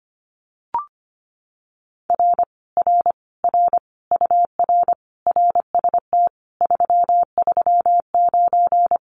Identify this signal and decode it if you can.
Morse code, heard as a keyed tone, about 25 wpm, 700 Hz. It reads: R R R UR RST 339
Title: Common QSO Phrases Encore 25wpm